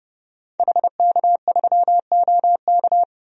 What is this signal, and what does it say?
Morse code, heard as a keyed tone, about 32 wpm, 700 Hz. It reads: HK3OK
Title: DXCC-100 Call Signs - 5 Characters - 30wpm